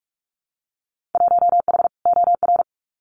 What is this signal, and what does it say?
Morse code, heard as a keyed tone, about 46 wpm, 700 Hz. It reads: 1H OR
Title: Winter Field Day Exchanges Encore 45wpm